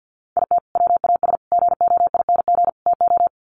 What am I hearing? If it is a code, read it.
Morse code, heard as a keyed tone, about 50 wpm, 700 Hz. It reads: IT WAS GOING TO
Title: Sets of 4 Words 50wpm